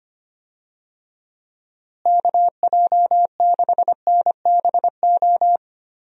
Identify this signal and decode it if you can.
Morse code, heard as a keyed tone, about 25 wpm, 700 Hz. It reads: KJ6NBO
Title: USA Call Signs - 6 Characters - Repeated 2x - 25wpm